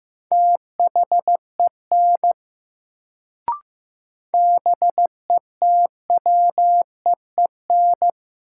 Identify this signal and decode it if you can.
Morse code, heard as a keyed tone, about 15 wpm, 700 Hz. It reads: THEN BETWEEN
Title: Sets of 2 Words - New Word Drill - 15wpm